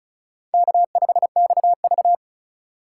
Morse code, heard as a keyed tone, about 34 wpm, 700 Hz. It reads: K5XV
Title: USA Call Signs - 4 Characters - 4x Character Spacing - 35wpm